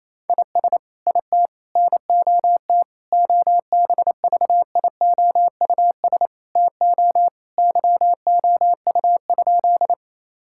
Morse code, heard as a keyed tone, about 28 wpm, 700 Hz. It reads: IS IT NOT OBVIOUS TO YOU?
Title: Sentences from Top 1600 Words 28wpm